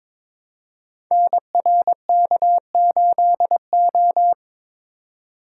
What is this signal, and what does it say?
Morse code, heard as a keyed tone, about 22 wpm, 700 Hz. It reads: NRK8O
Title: Code Groups - 5 Letter:Numbers - 22wpm